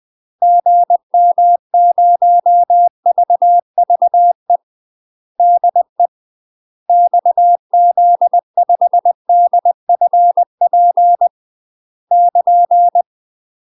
Morse code, heard as a keyed tone, about 20 wpm, 700 Hz. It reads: GM0VVE DE XZ5DFP <KN>
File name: UK MASTER QSOs TRAINING / 20WPM / UK MASTER QSOs 0062